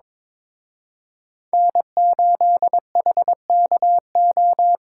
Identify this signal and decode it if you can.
Morse code, heard as a keyed tone, about 26 wpm, 700 Hz. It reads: N8HKO
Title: Speed Racing - USA 1x3 Call Signs 22wpm